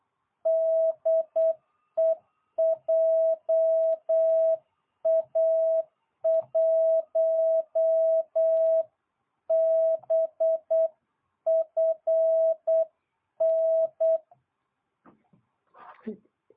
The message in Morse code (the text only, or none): DE JA1BFN